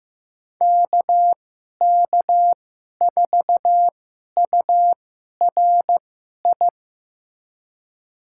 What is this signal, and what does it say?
Morse code, heard as a keyed tone, about 15 wpm, spaced as for 11 wpm, 700 Hz. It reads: KK4URI
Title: USA Call Signs - 6 Characters - 2x Character Spacing - 15wpm